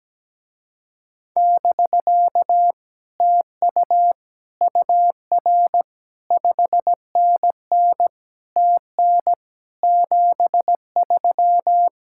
Morse code, heard as a keyed tone, about 17 wpm, 700 Hz. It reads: <BK> TU UR 5NN TN 73
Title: Speed-Racing Common QSO Phrases - 17wpm